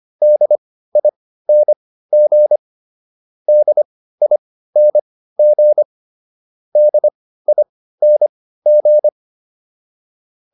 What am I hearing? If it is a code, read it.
Morse code, heard as a keyed tone, about 25 wpm, spaced as for 15 wpm, 600 Hz. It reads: DING DING DING